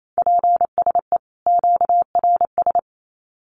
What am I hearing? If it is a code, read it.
Morse code, heard as a keyed tone, about 28 wpm, 700 Hz. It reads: PSE QRS
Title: Common QSO Phrases 28wpm